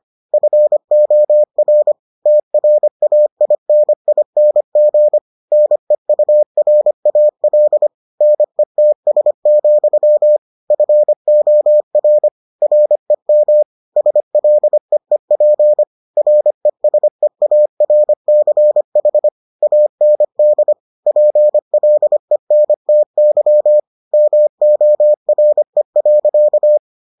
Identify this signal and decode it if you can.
Morse code, heard as a keyed tone, about 25 wpm, 600 Hz. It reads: FOR TRAINING NEURAL NETS, FOR REM SLEEP RESEARCH AND PLENTY MORE.